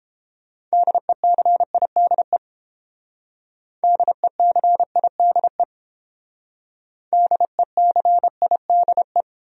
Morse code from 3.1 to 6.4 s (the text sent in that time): DECIDE